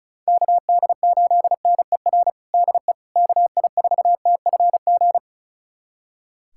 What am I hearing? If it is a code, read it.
Morse code, heard as a keyed tone, about 36 wpm, 700 Hz. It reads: KD8NER DE KI4TFG
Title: USA Callsign DE Callsign - 6 Characters - 35wpm